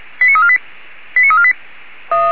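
Five-tone selective call (ZVEI), heard as quick runs of five tones, tones of about 70 ms. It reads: 89348 89348
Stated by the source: ZVEI five-tone selective call received by SDR.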